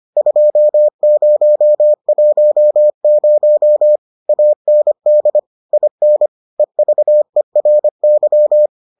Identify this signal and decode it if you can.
Morse code, heard as a keyed tone, about 25 wpm, 600 Hz. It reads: 2010 AND IN EVERY